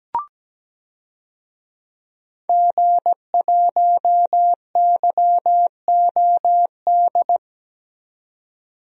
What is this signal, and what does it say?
Morse code, heard as a keyed tone, about 17 wpm, 700 Hz. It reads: G1YOD
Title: DXCC-100 Call Signs - 5 Characters - 17wpm